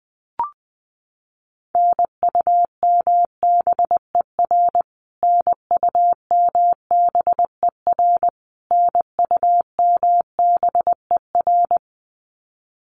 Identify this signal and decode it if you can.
Morse code, heard as a keyed tone, about 20 wpm, 700 Hz. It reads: NUMBER NUMBER NUMBER
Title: Top 200 Words Repeated 20wpm